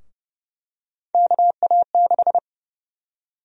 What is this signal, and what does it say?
Morse code, heard as a keyed tone, about 30 wpm, 700 Hz. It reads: KA6
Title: USA Call Sign Prefixes - Repeated 2x - 30wpm